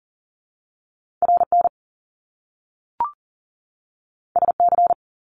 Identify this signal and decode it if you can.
Morse code, heard as a keyed tone, about 40 wpm, 700 Hz. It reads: RN SC
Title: Top Two Letter Combinations in Words 40wpm